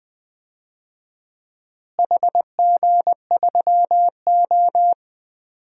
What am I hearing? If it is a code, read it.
Morse code, heard as a keyed tone, about 20 wpm, 700 Hz. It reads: HG3O
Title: DXCC-100 Call Signs - 4 Characters - Repeated 3x - 20wpm